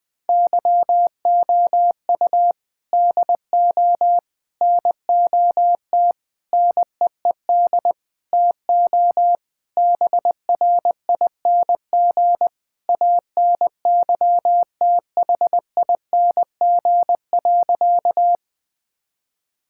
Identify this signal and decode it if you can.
Morse code, heard as a keyed tone, about 20 wpm, 700 Hz. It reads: YOU DO NOT NEED TO BRING ANYTHING.
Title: Sentences with 4-Word Phrases - 20wpm